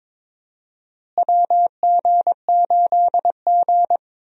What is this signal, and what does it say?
Morse code, heard as a keyed tone, about 22 wpm, 700 Hz. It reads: WG8G
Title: USA Call Signs - 4 Characters - Repeated 2x - 22wpm